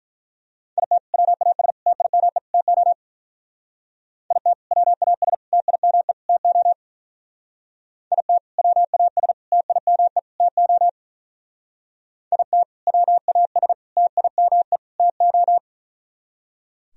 Morse code, heard as a keyed tone, about 44 wpm, 700 Hz. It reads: IT WAS TIME TO IT WAS TIME TO IT WAS TIME TO IT WAS TIME TO
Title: Speed-Racing - Sets of 4 Words 35wpm